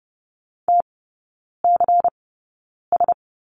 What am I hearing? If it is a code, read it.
Morse code, heard as a keyed tone, about 30 wpm, spaced as for 9 wpm, 700 Hz. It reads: TCS
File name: Lesson 16e - Letter c - Three Characters 30wpm